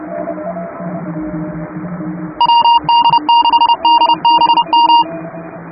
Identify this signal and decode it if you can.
Morse code, heard as a keyed tone, about 30 wpm, 940 Hz, with noise. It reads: WD6DBM
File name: chatham / 2350-12CW